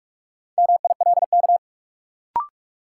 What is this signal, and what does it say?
Morse code, heard as a keyed tone, about 43 wpm, 700 Hz. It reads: MILK
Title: Sentences from Top 1500 Words - New Word Drill - 45wpm